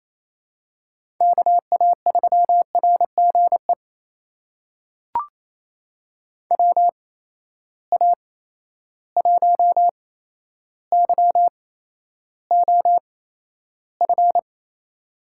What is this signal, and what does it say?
Morse code, heard as a keyed tone, about 28 wpm, spaced as for 12 wpm, 700 Hz. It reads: KA3RGE WA1YOF
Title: USA Call Signs - 6 Characters - 8x Character Spacing - 28wpm